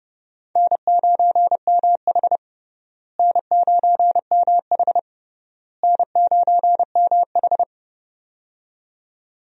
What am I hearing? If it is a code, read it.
Morse code, heard as a keyed tone, about 30 wpm, 700 Hz. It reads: N9MH N9MH N9MH
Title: USA Call Signs - 4 Characters - Repeated 3x - 30wpm